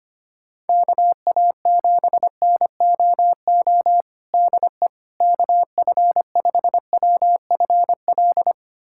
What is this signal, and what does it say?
Morse code, heard as a keyed tone, about 25 wpm, 700 Hz. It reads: KA7NOO DE KF5WFL
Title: USA Callsign DE Callsign 25wpm